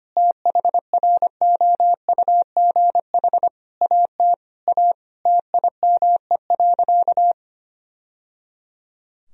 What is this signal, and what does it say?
Morse code, heard as a keyed tone, about 25 wpm, 700 Hz. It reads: THROUGH AT A TIME.
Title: Sentences from Top 1000 Words - Review Part 1 - 25wpm